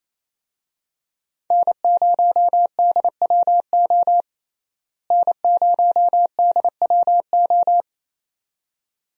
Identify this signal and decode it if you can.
Morse code, heard as a keyed tone, about 27 wpm, 700 Hz. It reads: N0DWO N0DWO
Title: USA Call Signs - 5 Characters - Repeated 2x - 28wpm